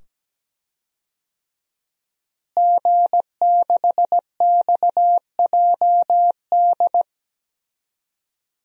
Morse code, heard as a keyed tone, about 17 wpm, 700 Hz. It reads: G6XJD
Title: DXCC-100 Call Signs - 5 Characters - Repeated 2x - 17wpm